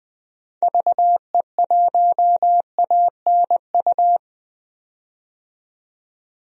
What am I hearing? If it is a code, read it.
Morse code, heard as a keyed tone, about 20 wpm, 700 Hz. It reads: VE1ANU
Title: DXCC-100 Call Signs - 6 Characters - 20wpm